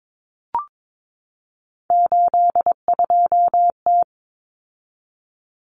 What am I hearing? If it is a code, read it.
Morse code, heard as a keyed tone, about 22 wpm, 700 Hz. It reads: 82T
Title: Common Cut Numbers - 3-Digit 22wpm